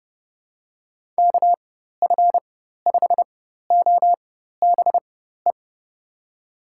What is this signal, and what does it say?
Morse code, heard as a keyed tone, about 30 wpm, spaced as for 14 wpm, 700 Hz. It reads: KF5OBE
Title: USA Call Signs - 6 Characters - 4x Character Spacing - 30wpm